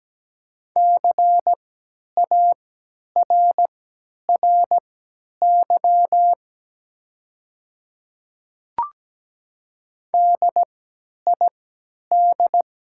Morse code, heard as a keyed tone, about 17 wpm, spaced as for 10 wpm, 700 Hz. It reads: CARRY DID
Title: 3x Character Spacing - Top 500 Words - 17wpm